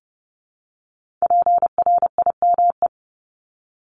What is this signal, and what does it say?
Morse code, heard as a keyed tone, about 30 wpm, 700 Hz. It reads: PRIME
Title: Multi-letter Phonemes 30wpm